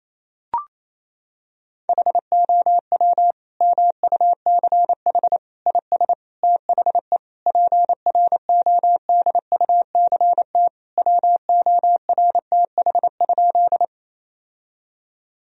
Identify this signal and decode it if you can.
Morse code, heard as a keyed tone, about 28 wpm, 700 Hz. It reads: HOW MUCH IS THE PRODUCT WORTH?
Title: Sentences from Top 1000 Words - Review Part 2 - 28wpm